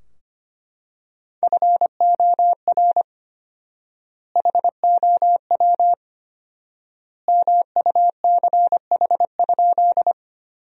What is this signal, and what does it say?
Morse code, heard as a keyed tone, about 25 wpm, 700 Hz. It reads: FOR HOW MUCH?
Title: 4x Word-Spacing - Sentences from Top 200 Words - 25wpm